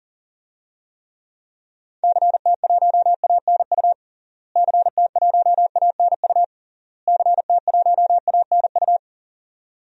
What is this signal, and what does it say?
Morse code, heard as a keyed tone, about 40 wpm, 700 Hz. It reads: CT1ANU CT1ANU CT1ANU
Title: DXCC-100 Call Signs - 6 Characters - Repeated 3x - 40wpm